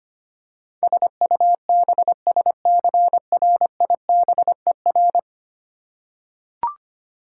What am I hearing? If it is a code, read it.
Morse code, heard as a keyed tone, about 25 wpm, 700 Hz. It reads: SUBSCRIBER
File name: 10 Letter Words - First 3 Letters Given - 25wpm